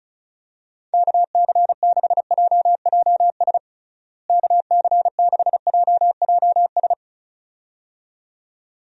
Morse code, heard as a keyed tone, about 36 wpm, 700 Hz. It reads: KC6JJS KC6JJS
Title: USA Call Signs - 6 Characters - Repeated 2x - 35wpm